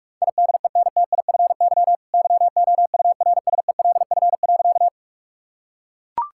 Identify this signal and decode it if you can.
Morse code, heard as a keyed tone, about 42 wpm, 700 Hz. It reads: IDENTIFY YOURSELF.
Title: Sentences from Top 1000 Words - Review Part 6 - 45wpm